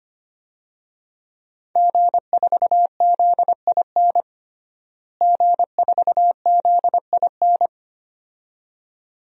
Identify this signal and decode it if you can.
Morse code, heard as a keyed tone, about 25 wpm, 700 Hz. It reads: G4ZIN G4ZIN
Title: DXCC-100 Call Signs - 5 Characters - Repeated 2x - 25wpm